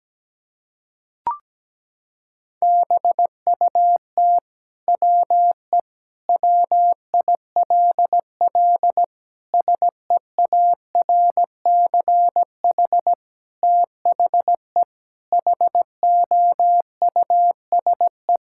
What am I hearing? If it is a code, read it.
Morse code, heard as a keyed tone, about 17 wpm, 700 Hz. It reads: BUT WE WILL SEARCH THE HOUSE
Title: Sentences from Top 1000 Words - Review Part 5 - 17wpm